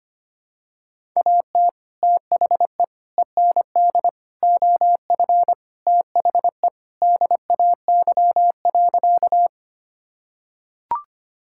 Morse code, heard as a keyed tone, about 25 wpm, 700 Hz. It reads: AT THE END OF THE DAY.
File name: practice / 25wpm / Sentences with 2-Word Phrases - 25wpm